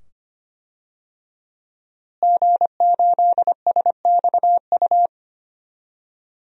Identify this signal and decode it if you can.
Morse code, heard as a keyed tone, about 25 wpm, 700 Hz. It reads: G8SXU
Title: DXCC-100 Call Signs - 5 Characters - Repeated 2x - 25wpm